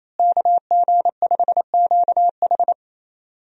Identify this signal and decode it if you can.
Morse code, heard as a keyed tone, about 28 wpm, 700 Hz. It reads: KG5QH
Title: USA Call Signs - 5 Characters - Repeated 3x - 28wpm